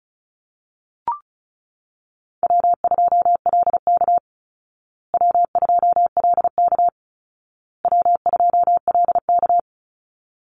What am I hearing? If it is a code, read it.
Morse code, heard as a keyed tone, about 34 wpm, 700 Hz. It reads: W2LK W2LK W2LK
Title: Easy Call Signs Repeated 3x - 35wpm